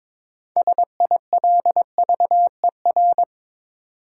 Morse code, heard as a keyed tone, about 21 wpm, 700 Hz. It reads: SILVER
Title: Six Letter Words - 22wpm